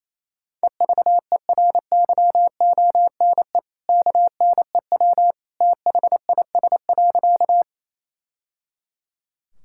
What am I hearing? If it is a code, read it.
Morse code, heard as a keyed tone, about 28 wpm, 700 Hz. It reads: EVERYONE KNEW THIS.
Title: Sentences from Top 1000 Words - Review Part 5 - 28wpm